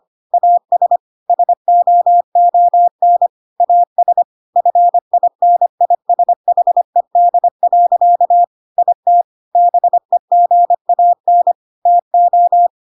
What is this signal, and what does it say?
Morse code, heard as a keyed tone, about 25 wpm, 700 Hz. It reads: AS SOON AS FINISHED. IT BEGAN TO
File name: Pinocchio / Pinocchio ch3 25 25 ik1hge